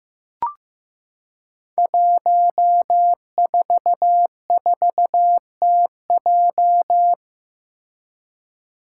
Morse code, heard as a keyed tone, about 15 wpm, 700 Hz. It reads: 144TJ
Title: Five Challenging Letter-Number Combinations 15wpm